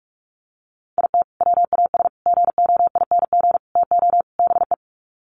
Morse code, heard as a keyed tone, about 44 wpm, 700 Hz. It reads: IT WAS GOING TO BE
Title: Sets of 5 Words 45wpm